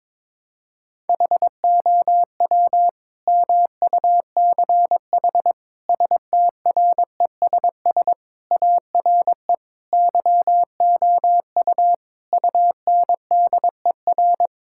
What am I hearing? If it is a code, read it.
Morse code, heard as a keyed tone, about 21 wpm, 700 Hz. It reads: HOW MUCH STRESS ARE YOU UNDER
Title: Sentences from Top 1000 Words - Review Part 6 - 22wpm